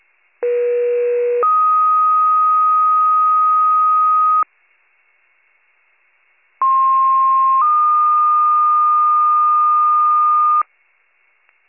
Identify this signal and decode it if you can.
Two-tone page, heard as a one-second tone, then a three-second tone, about 480 then 1250 Hz, about 1060 then 1220 Hz. It is heard from 0.4 to 10.6 s.